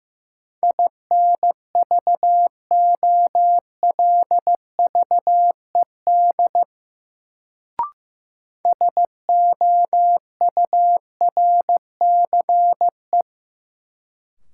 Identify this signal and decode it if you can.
Morse code, heard as a keyed tone, about 15 wpm, 700 Hz. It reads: INVOLVED SOURCE
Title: Sentences from Top 800 Words - New Word Drill - 15wpm